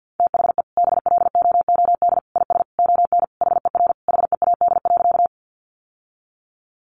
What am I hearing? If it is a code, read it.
Morse code, heard as a keyed tone, about 50 wpm, 700 Hz. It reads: THE BLOOD IS ON HER HEAD.
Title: Sentences from Top 1100 Words 50wpm